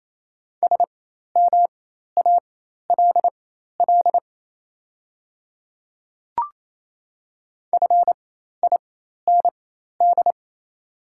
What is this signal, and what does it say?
Morse code, heard as a keyed tone, about 28 wpm, spaced as for 13 wpm, 700 Hz. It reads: SMALL FIND
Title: 4x Character Spacing - Top 300 Words - 28wpm